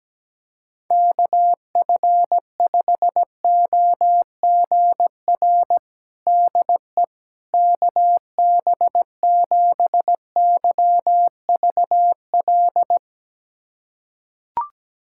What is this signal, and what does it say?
Morse code, heard as a keyed tone, about 17 wpm, 700 Hz. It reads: KF5OGR DE KB7YVL
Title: USA Callsign DE Callsign - 6 Characters - 17wpm